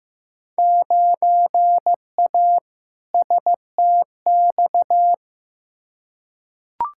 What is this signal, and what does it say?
Morse code, heard as a keyed tone, about 15 wpm, 700 Hz. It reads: 9A STX
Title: Field Day Exchanges - 15wpm